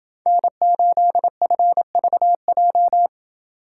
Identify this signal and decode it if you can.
Morse code, heard as a keyed tone, about 27 wpm, 700 Hz. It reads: N8FVJ